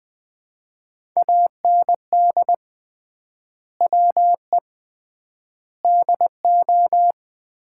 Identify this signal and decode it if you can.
Morse code, heard as a keyed tone, about 20 wpm, 700 Hz. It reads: AND WE DO
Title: 3x Word-Spacing - Sets of 4 Words - 20wpm